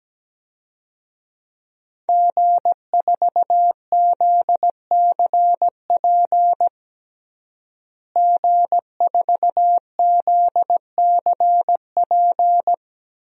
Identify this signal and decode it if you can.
Morse code, heard as a keyed tone, about 17 wpm, 700 Hz. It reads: G4ZCP G4ZCP
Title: DXCC-100 Call Signs - 5 Characters - Repeated 2x - 17wpm